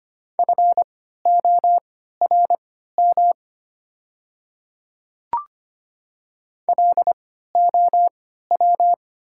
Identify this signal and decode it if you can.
Morse code, heard as a keyed tone, about 25 wpm, spaced as for 14 wpm, 700 Hz. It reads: FORM LOW
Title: 3x Character Spacing - Top 200 Words - 25wpm